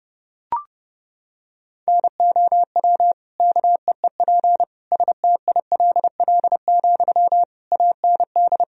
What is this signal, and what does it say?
Morse code, heard as a keyed tone, about 30 wpm, 700 Hz. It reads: NOW KEEP STILL, AND
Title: Sentences from Top 1000 Words - Review Part 2 - 30wpm